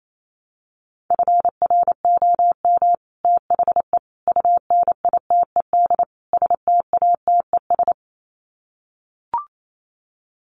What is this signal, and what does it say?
Morse code, heard as a keyed tone, about 28 wpm, 700 Hz. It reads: FROM THE UNITED STATES
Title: Sets of 4 Words 28wpm